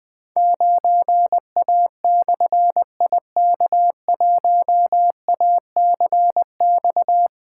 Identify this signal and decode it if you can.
Morse code, heard as a keyed tone, about 20 wpm, 700 Hz. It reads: 9A/IK1ACX
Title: Difficult Call Signs 20wpm